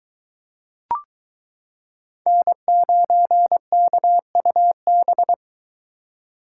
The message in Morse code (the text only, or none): N9KUB